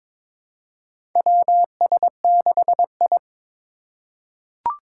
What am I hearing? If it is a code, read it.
Morse code, heard as a keyed tone, about 22 wpm, 700 Hz. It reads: WS6I
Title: USA Call Signs - 4 Characters - 3x Character Spacing - 22wpm